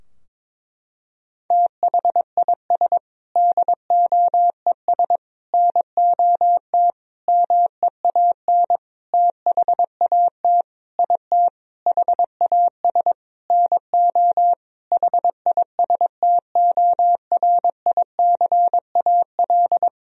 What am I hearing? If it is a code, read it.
Morse code, heard as a keyed tone, about 22 wpm, 700 Hz. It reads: THIS DOES NOT MEAN THAT IT HAS NO HISTORICAL
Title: Sentences from Top 1000 Words - Review Part 4 - 22wpm